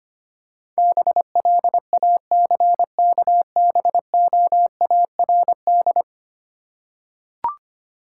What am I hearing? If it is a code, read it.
Morse code, heard as a keyed tone, about 25 wpm, 700 Hz. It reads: BLACKBOARD
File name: practice / 25wpm / Easy Words with Hints - 25wpm